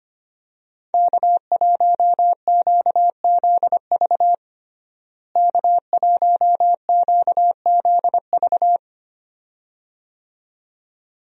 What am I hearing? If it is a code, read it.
Morse code, heard as a keyed tone, about 25 wpm, 700 Hz. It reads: K1QZV K1QZV
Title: USA Call Signs - 5 Characters - Repeated 2x - 25wpm